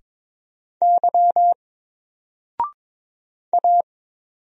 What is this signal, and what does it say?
Morse code, heard as a keyed tone, about 22 wpm, 700 Hz. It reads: Y A
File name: Single-Letter-Number-22wpm